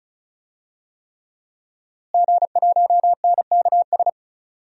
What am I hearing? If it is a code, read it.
Morse code, heard as a keyed tone, about 34 wpm, 700 Hz. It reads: G1NKS
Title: DXCC-100 Call Signs - 5 Characters - Repeated 3x - 35wpm